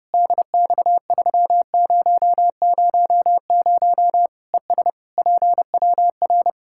Morse code, heard as a keyed tone, about 30 wpm, 700 Hz. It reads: DX3000 ES PWR
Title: Practice QSO - Exchange 2 - Info - 30wpm